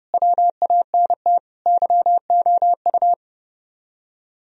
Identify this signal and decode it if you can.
Morse code, heard as a keyed tone, about 38 wpm, 700 Hz. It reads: WANT YOU
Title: Speed-Racing - Sets of 5 Words 30wpm